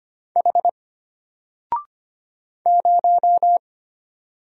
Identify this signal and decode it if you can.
Morse code, heard as a keyed tone, about 25 wpm, 700 Hz. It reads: H 0
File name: Single-Letter-Number-25wpm